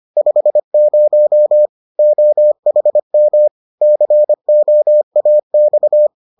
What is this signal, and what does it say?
Morse code, heard as a keyed tone, about 25 wpm, 600 Hz. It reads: 50 OHM COAX